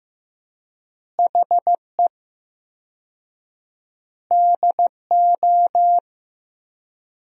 Morse code, heard as a keyed tone, about 16 wpm, 700 Hz. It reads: HE DO
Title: 4x Word-Spacing - Sentences from Top 100 Words - 15wpm